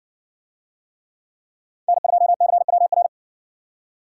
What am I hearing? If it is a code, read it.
Morse code, heard as a keyed tone, about 55 wpm, 700 Hz. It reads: N2CWR